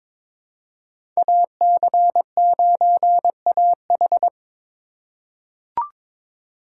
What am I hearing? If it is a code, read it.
Morse code, heard as a keyed tone, about 22 wpm, 700 Hz. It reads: AC9AH
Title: USA Call Signs - 5 Characters - 2x Character Spacing - 22wpm